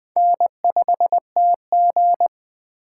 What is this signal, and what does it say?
Morse code, heard as a keyed tone, about 20 wpm, 700 Hz. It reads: N5TG